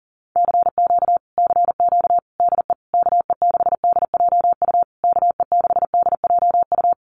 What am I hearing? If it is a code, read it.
Morse code, heard as a keyed tone, about 40 wpm, 700 Hz. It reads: CQ CQ DE KE6DJU KE6DJU
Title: Common QSO Phrases Encore 40wpm